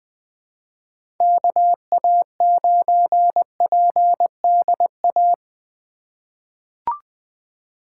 Morse code, heard as a keyed tone, about 20 wpm, 700 Hz. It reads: KA9PDA
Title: USA Call Signs - 6 Characters - 2x Character Spacing - 20wpm